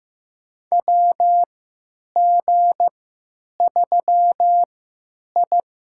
Morse code, heard as a keyed tone, about 15 wpm, spaced as for 9 wpm, 700 Hz. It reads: WG3I
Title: USA Call Signs - 4 Characters - 3x Character Spacing - 15wpm